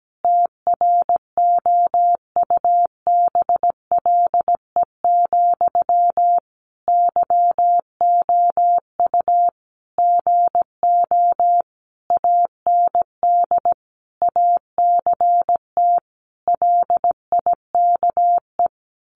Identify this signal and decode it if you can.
Morse code, heard as a keyed tone, about 17 wpm, 700 Hz. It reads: TROUBLE, YOU GO AND ACT LIKE
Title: Sentences from Top 1200 Words 17wpm